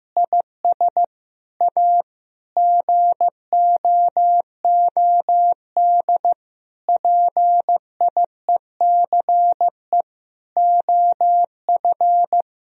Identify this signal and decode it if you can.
Morse code, heard as a keyed tone, about 15 wpm, 700 Hz. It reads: IS A GOOD PIECE OF
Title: Sentences from Top 1000 Words - Review Part 2 - 15wpm